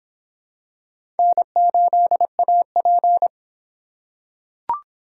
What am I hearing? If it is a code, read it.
Morse code, heard as a keyed tone, about 26 wpm, 700 Hz. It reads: N8AP